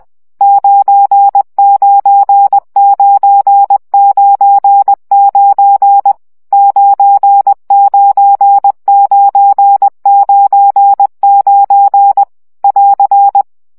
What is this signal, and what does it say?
Morse code, heard as a keyed tone, about 20 wpm, 800 Hz. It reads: 99999 99999 <AR>
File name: CW gyakorlas 9 17 baud